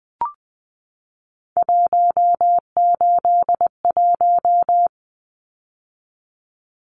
Morse code, heard as a keyed tone, about 20 wpm, 700 Hz. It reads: 181
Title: Numbers - 3-Digit 20wpm